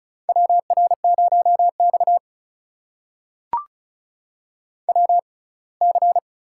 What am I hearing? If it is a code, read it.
Morse code, heard as a keyed tone, about 36 wpm, spaced as for 24 wpm, 700 Hz. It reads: WR0X WC
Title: USA Call Signs - 4 Characters - 6x Character Spacing - 35wpm